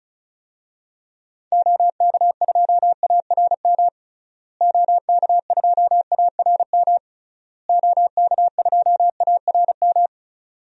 Morse code, heard as a keyed tone, about 35 wpm, 700 Hz. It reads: OK2ARM OK2ARM OK2ARM
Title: DXCC-100 Call Signs - 6 Characters - Repeated 3x - 35wpm